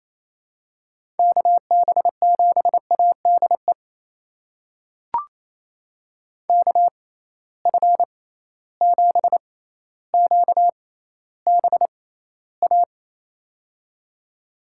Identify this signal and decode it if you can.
Morse code, heard as a keyed tone, about 28 wpm, spaced as for 14 wpm, 700 Hz. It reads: KB7ADE KF7QBA
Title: USA Call Signs - 6 Characters - 6x Character Spacing - 28wpm